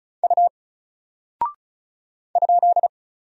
Morse code, heard as a keyed tone, about 36 wpm, 700 Hz. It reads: U ?
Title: Single Character (Letter:Number:Punctuation:Prosign) 35wpm